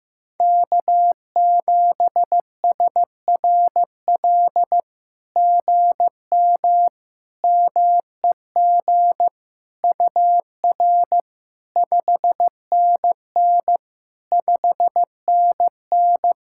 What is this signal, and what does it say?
Morse code, heard as a keyed tone, about 16 wpm, 700 Hz. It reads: K7SRL GM MEG UR 5NN 5NN
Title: SOTA QSO Practice 15wpm